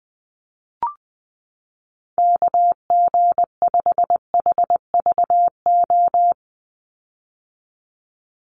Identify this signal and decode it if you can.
Morse code, heard as a keyed tone, about 19 wpm, 700 Hz. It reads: KG5HVO
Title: Call Signs - 20wpm